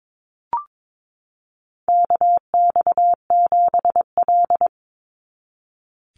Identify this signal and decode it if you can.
Morse code, heard as a keyed tone, about 22 wpm, 700 Hz. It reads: KX7L
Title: NA SOTA Chaser Callsigns 22wpm